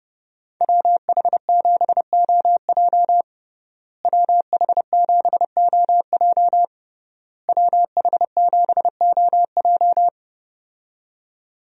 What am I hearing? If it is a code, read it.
Morse code, heard as a keyed tone, about 30 wpm, 700 Hz. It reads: WH7OJ WH7OJ WH7OJ
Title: USA Call Signs - 5 Characters - Repeated 3x - 30wpm